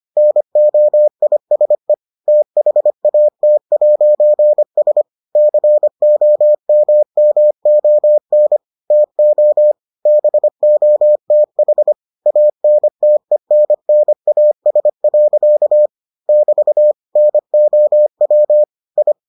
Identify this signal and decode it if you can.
Morse code, heard as a keyed tone, about 25 wpm, 600 Hz. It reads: NOISE THAT'S COMMON TO BOTH ANTENNAS. <BT> NOW I